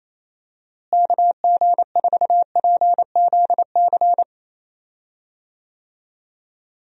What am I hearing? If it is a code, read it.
Morse code, heard as a keyed tone, about 28 wpm, 700 Hz. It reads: KG4PZC
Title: USA Call Signs - 6 Characters - Repeated 3x - 28wpm